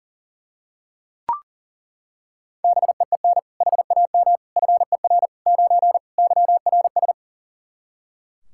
Morse code, heard as a keyed tone, about 37 wpm, 700 Hz. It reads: BEEN HAM FER 9 YRS
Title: Practice QSO - Exchange 3 - More Info - 40wpm